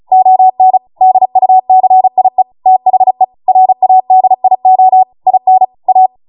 Morse code, heard as a keyed tone, about 36 wpm, 750 Hz. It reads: ON DUCIE THE RADIO IN A